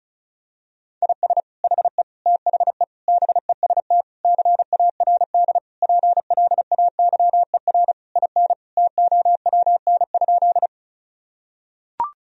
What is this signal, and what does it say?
Morse code, heard as a keyed tone, about 36 wpm, 700 Hz. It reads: IS HE THE BEST CARD PLAYER IN TOWN?
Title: Sentences from Top 1000 Words - Review Part 4 - 35wpm